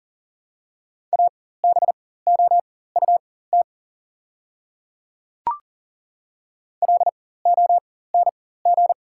Morse code, heard as a keyed tone, about 40 wpm, spaced as for 19 wpm, 700 Hz. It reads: ABOUT LONG
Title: 4x Character Spacing - Top 100 Words - 40wpm